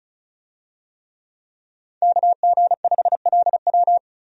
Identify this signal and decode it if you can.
Morse code, heard as a keyed tone, about 36 wpm, 700 Hz. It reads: KG5LW
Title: USA Call Signs - 5 Characters - 35wpm